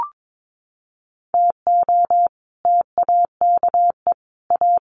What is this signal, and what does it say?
Morse code, heard as a keyed tone, about 22 wpm, 700 Hz. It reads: TO TAKE A
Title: Sets of 5 Words Encore2 22wpm